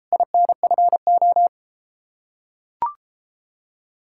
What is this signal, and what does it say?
Morse code, heard as a keyed tone, about 33 wpm, 700 Hz. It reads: INFO